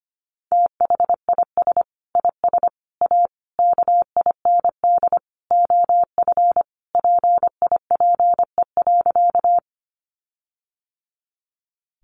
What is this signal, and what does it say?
Morse code, heard as a keyed tone, about 25 wpm, 700 Hz. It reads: THIS IS A KIND OF PIPE.
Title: Sentences from Top 1500 Words 25wpm